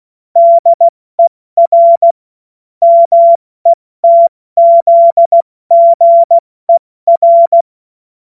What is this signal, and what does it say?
Morse code, heard as a keyed tone, about 16 wpm, 680 Hz. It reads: DER METZGER
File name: Morsekurs / Witze / Witz 04. Oktober 2022 70 F80